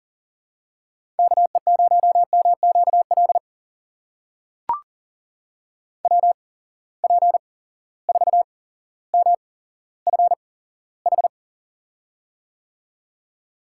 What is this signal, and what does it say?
Morse code, heard as a keyed tone, about 40 wpm, spaced as for 17 wpm, 700 Hz. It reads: KE0MQL WP4MFH
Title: USA Call Signs - 6 Characters - 8x Character Spacing - 40wpm